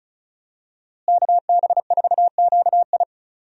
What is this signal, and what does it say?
Morse code, heard as a keyed tone, about 36 wpm, 700 Hz. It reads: KB4QI
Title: USA Call Signs - 5 Characters - Repeated 3x - 35wpm